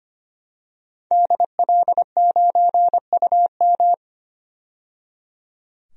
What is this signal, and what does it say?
Morse code, heard as a keyed tone, about 25 wpm, 700 Hz. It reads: DL9UM
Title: DXCC-100 Call Signs - 5 Characters - 25wpm